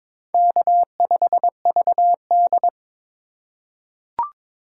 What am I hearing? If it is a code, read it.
Morse code, heard as a keyed tone, about 23 wpm, 700 Hz. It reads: K5VD
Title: USA Call Signs - 3x Character Spacing - 22wpm